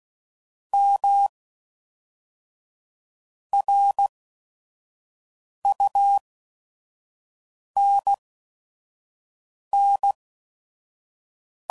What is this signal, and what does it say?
Morse code, heard as a keyed tone, about 16 wpm, spaced as for 5 wpm, 800 Hz. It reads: M RUNN
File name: CW Code Groups @5WPM / QSOs / kc7haa